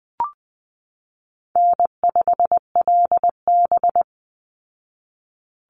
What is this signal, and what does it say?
Morse code, heard as a keyed tone, about 20 wpm, 700 Hz. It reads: N5LB
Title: Easy Call Signs Repeated 3x - 20wpm